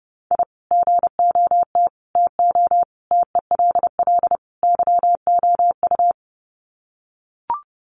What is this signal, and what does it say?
Morse code, heard as a keyed tone, about 30 wpm, 700 Hz. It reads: I GOT TO TELL YOU
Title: Sets of 5 Words Encore2 30wpm